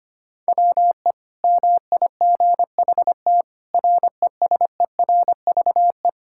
Morse code, heard as a keyed tone, about 25 wpm, 700 Hz. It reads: WE MIGHT RESERVE
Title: Sentences from Top 1000 Words - Review Part 6 - 25wpm